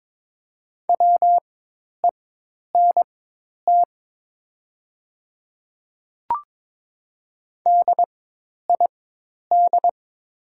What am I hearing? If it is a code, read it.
Morse code, heard as a keyed tone, about 22 wpm, spaced as for 10 wpm, 700 Hz. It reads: WENT DID
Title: 4x Character Spacing - Top 300 Words - 22wpm